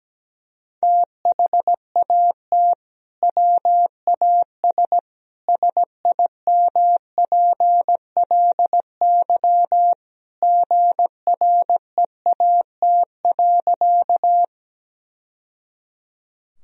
Morse code, heard as a keyed tone, about 17 wpm, 700 Hz. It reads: THAT WAS SIMPLY GREAT.
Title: Sentences from Top 1000 Words - Review Part 4 - 17wpm